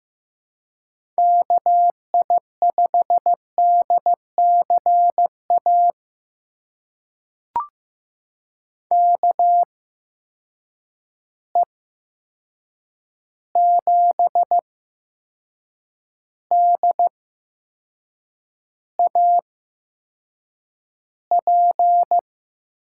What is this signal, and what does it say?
Morse code, heard as a keyed tone, about 15 wpm, spaced as for 6 wpm, 700 Hz. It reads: KI5DCA KE7DAP
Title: USA Call Signs - 6 Characters - 8x Character Spacing - 15wpm